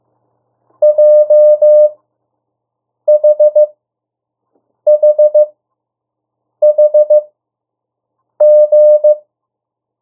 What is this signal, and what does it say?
Morse code, heard as a keyed tone, about 15 wpm, spaced as for 6 wpm, 600 Hz. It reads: JHHHG
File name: urok 1